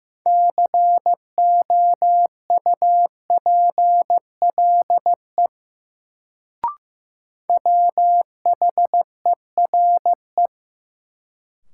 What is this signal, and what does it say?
Morse code, heard as a keyed tone, about 15 wpm, 700 Hz. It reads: COUPLE WHERE
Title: Sets of 2 Words - New Word Drill - 15wpm